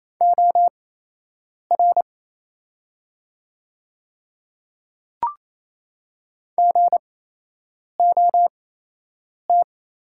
Morse code, heard as a keyed tone, about 28 wpm, spaced as for 8 wpm, 700 Hz. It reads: OR GOT
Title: 8x Character Spacing - Top 300 Words - 28wpm